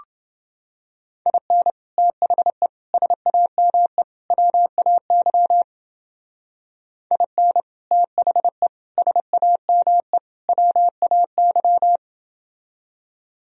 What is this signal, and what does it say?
Morse code, heard as a keyed tone, about 28 wpm, 700 Hz. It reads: IN THE SAME WAY IN THE SAME WAY